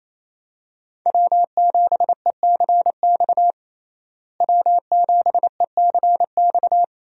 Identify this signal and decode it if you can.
Morse code, heard as a keyed tone, about 28 wpm, 700 Hz. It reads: W7ECX W7ECX
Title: USA Call Signs - 5 Characters - Repeated 2x - 28wpm